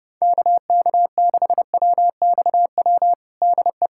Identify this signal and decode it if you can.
Morse code, heard as a keyed tone, about 30 wpm, 700 Hz. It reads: KK6WXW DE
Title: USA Callsign DE Callsign - 6 Characters - 30wpm